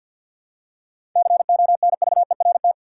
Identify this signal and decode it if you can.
Morse code, heard as a keyed tone, about 51 wpm, 700 Hz. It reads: CONVERT
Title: Seven Letter Words - 50wpm